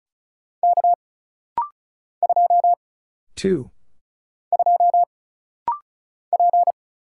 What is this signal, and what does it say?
Morse code, heard as a keyed tone, about 35 wpm, 700 Hz. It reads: K 2 2 P
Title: Single Letter-Number Alt - Rapid-Fire 35wpm